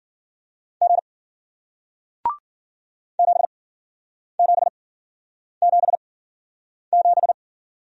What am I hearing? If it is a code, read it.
Morse code, heard as a keyed tone, about 54 wpm, spaced as for 9 wpm, 700 Hz. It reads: G 7777